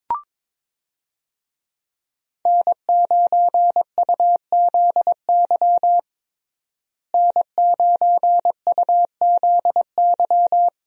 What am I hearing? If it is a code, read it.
Morse code, heard as a keyed tone, about 22 wpm, 700 Hz. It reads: N9UZY N9UZY
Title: USA Call Signs - 5 Characters - Repeated 3x - 22wpm